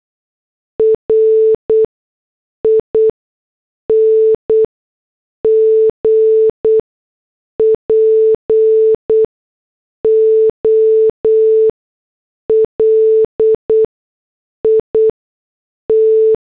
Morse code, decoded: RING POLIT